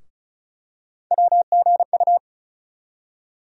Morse code, heard as a keyed tone, about 36 wpm, 700 Hz. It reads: WGU
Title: USA Call Sign Postfixes - 3 Letters 35wpm